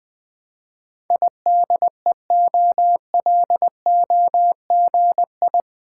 Morse code, heard as a keyed tone, about 21 wpm, 700 Hz. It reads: IDEOLOGI
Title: Build A Word - 4 Letters At A Time - 11-Letter Words - 20wpm